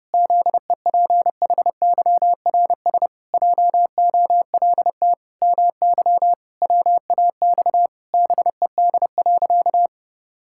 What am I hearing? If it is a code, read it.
Morse code, heard as a keyed tone, about 30 wpm, 700 Hz. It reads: ZEPHYRS JOLT MY WAX BED.
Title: Pangram Challenge v2 - 30wpm